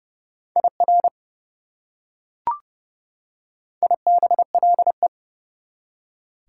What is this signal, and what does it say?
Morse code, heard as a keyed tone, about 30 wpm, 700 Hz. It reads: IR IBLE
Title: Prefixes and Suffixes 30wpm